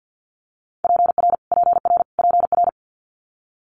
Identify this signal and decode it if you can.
Morse code, heard as a keyed tone, about 50 wpm, 700 Hz. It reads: PR PR PR
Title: Multi-letter Phonemes Repeated 3x 50wpm